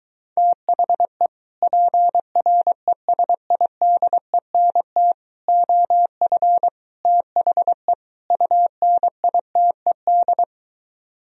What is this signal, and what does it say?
Morse code, heard as a keyed tone, about 23 wpm, 700 Hz. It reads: THE PRESIDENT OF THE UNITED